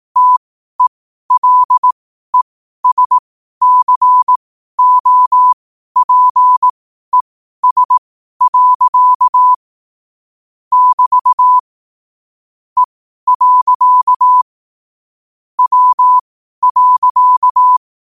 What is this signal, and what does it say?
Morse code, heard as a keyed tone, about 18 wpm, spaced as for 12 wpm, 1000 Hz. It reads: TELESCOPES. <BT> E. W.
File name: qotdfiles / 12wpm